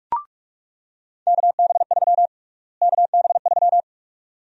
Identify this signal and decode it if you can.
Morse code, heard as a keyed tone, about 42 wpm, 700 Hz. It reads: KB3 KB3
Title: USA Call Sign Prefixes - Repeated 2x - 45wpm